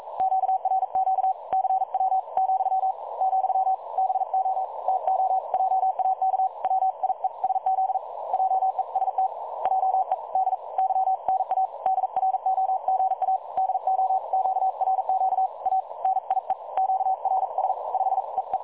Morse code, heard as a keyed tone, about 42 wpm, spaced as for 37 wpm, 760 Hz. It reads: QRQ CW ,, BO TO JAK WIESZ JEST JEDYNA DROGA DO QRQ A NIE QRS 5 EE
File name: dlazaawansowanych / nagraniazpasma / websdr recording 3565.0kHz